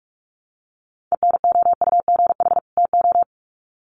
Morse code, heard as a keyed tone, about 44 wpm, 700 Hz. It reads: ENOUGH TO
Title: Sets of 2 Words 45wpm